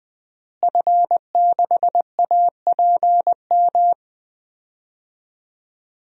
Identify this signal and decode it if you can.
Morse code, heard as a keyed tone, about 20 wpm, 700 Hz. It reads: F6APM
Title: DXCC-100 Call Signs - 5 Characters - Repeated 3x - 20wpm